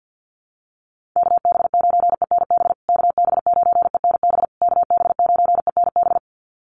Morse code, heard as a keyed tone, about 50 wpm, 700 Hz. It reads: KB9ENB KB9ENB KB9ENB
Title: USA Call Signs Repeated 3x 50wpm